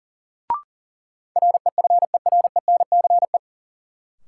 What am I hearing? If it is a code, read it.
Morse code, heard as a keyed tone, about 37 wpm, 700 Hz. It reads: REFERENCE
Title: Sentences from Top 1200 Words - New Word Drill - 40wpm